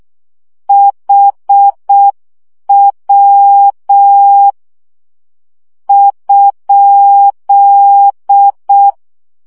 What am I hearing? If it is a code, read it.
Morse code, heard as a keyed tone, about 6 wpm, spaced as for 5 wpm, 800 Hz. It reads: HW?
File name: morze vizsga 2011-1 5